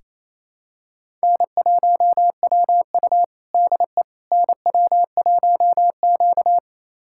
Morse code, heard as a keyed tone, about 28 wpm, 700 Hz. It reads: N1WU DE NW1Q
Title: USA Callsign DE Callsign - 4 Characters - 28wpm